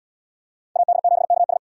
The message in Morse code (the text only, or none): AD7KS